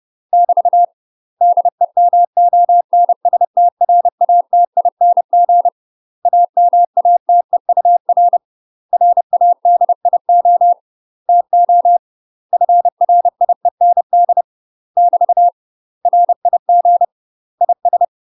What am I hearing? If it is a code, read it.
Morse code, heard as a keyed tone, about 30 wpm, 700 Hz. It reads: <BT> DEMONSTRATING AMATEUR RADIO TO FRIEND <BT> RIG IS
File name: UK MASTER QSOs TRAINING / 30WPM / UK MASTER QSOs 0002